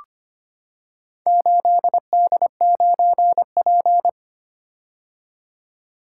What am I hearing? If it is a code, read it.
Morse code, heard as a keyed tone, about 25 wpm, 690 Hz. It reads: 8D9P
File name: practice / 25wpm / Four Letter-Number Combinations - 25wpm